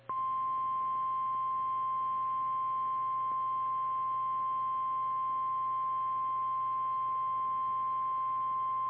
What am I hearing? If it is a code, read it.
Emergency Alert System (SAME) tones (the 1050 Hz attention tone). It is heard from 0.5 to 8.0 s.
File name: eas / KKPRKXPN / KKPRKXPN